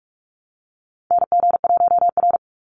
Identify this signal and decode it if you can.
Morse code, heard as a keyed tone, about 44 wpm, 700 Hz. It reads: NG1R
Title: Call Signs Repeated 2x - 45wpm